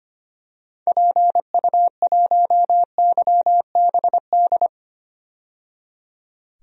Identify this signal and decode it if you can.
Morse code, heard as a keyed tone, about 25 wpm, 700 Hz. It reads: PU1YBD
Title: DXCC-100 Call Signs - 6 Characters - 25wpm